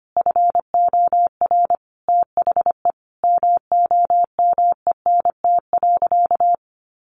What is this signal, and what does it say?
Morse code, heard as a keyed tone, about 25 wpm, 700 Hz. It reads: FOR THE MOMENT.
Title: Sentences from Top 1200 Words 25wpm